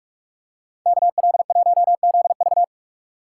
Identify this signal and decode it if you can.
Morse code, heard as a keyed tone, about 44 wpm, 700 Hz. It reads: KL1ZV